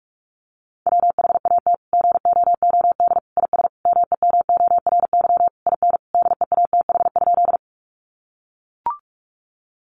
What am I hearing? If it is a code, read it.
Morse code, heard as a keyed tone, about 44 wpm, 700 Hz. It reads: WHAT GOOD IS MEMORY IN DEATH?
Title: Sentences from Top 1300 Words 45wpm